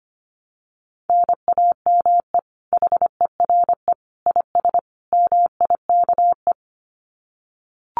Morse code, heard as a keyed tone, about 25 wpm, 700 Hz. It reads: NAME HERE IS MIKE
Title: Common QSO Phrases 25wpm